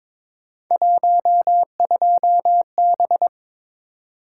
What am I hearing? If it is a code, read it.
Morse code, heard as a keyed tone, about 22 wpm, 700 Hz. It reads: 12B
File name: practice / 22wpm / Three Letter-Number Combinations - 22wpm